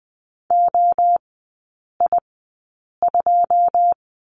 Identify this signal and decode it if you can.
Morse code, heard as a keyed tone, about 20 wpm, spaced as for 8 wpm, 690 Hz. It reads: OI2
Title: 191 Lesson 31e - Letter v - Three Characters 20wpm